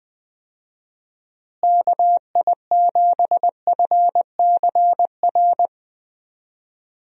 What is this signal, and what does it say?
Morse code, heard as a keyed tone, about 20 wpm, 700 Hz. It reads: KI7FCR
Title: USA Call Signs - 6 Characters - Repeated 2x - 20wpm